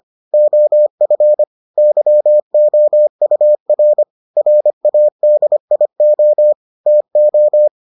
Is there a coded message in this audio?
Yes: Morse code from 0.3 to 7.7 s, heard as a keyed tone, about 25 wpm, 600 Hz. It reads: OF YOUR RADIO TO